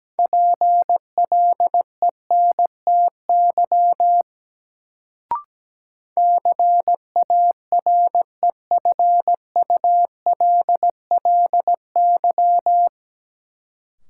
Morse code, heard as a keyed tone, about 17 wpm, 700 Hz. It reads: PLENTY CAREFULLY
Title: Sentences from Top 1300 Words - New Word Drill - 17wpm